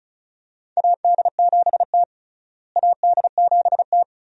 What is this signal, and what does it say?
Morse code, heard as a keyed tone, about 34 wpm, 700 Hz. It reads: AD7T AD7T
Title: USA Call Signs - 4 Characters - Repeated 2x - 35wpm